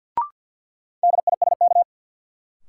Morse code, heard as a keyed tone, about 52 wpm, 700 Hz. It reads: DISK
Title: Sentences from Top 1000 Words - Review Part 6 - New Word Drill - 50wpm